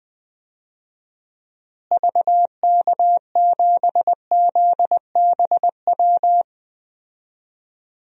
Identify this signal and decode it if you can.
Morse code, heard as a keyed tone, about 20 wpm, 700 Hz. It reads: VK7ZBW
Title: DXCC-100 Call Signs - 6 Characters - Repeated 2x - 20wpm